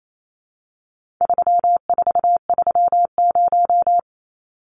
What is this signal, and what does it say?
Morse code, heard as a keyed tone, about 28 wpm, 700 Hz. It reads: 3430
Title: Numbers - 4-Digit 28wpm